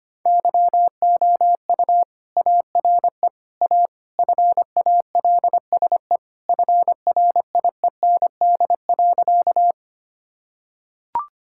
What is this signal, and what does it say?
Morse code, heard as a keyed tone, about 25 wpm, 700 Hz. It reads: YOU ARE A FALSE FRIEND.
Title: Sentences from Top 1600 Words 25wpm